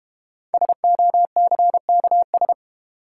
Morse code, heard as a keyed tone, about 32 wpm, 690 Hz. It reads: SOCKS